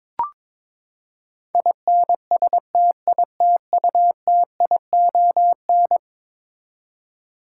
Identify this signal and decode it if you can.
Morse code, heard as a keyed tone, about 22 wpm, 700 Hz. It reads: INSTITUTION
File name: practice / 22wpm / Eleven Letter Words - 22wpm